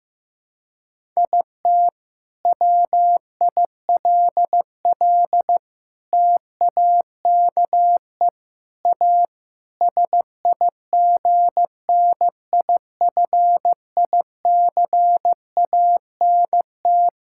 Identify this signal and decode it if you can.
Morse code, heard as a keyed tone, about 15 wpm, 700 Hz. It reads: IT WILL TAKE A SIGNIFICANT
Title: Sentences from Top 1000 Words - Review Part 6 - 15wpm